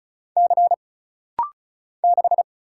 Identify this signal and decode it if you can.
Morse code, heard as a keyed tone, about 36 wpm, 700 Hz. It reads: C 6
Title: Single Letter-Number - Rapid-Fire 35wpm